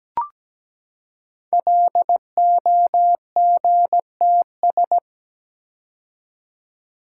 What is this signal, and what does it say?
Morse code, heard as a keyed tone, about 17 wpm, 700 Hz. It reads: LOGTS
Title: Code Groups - 5 Letter:Numbers - 17wpm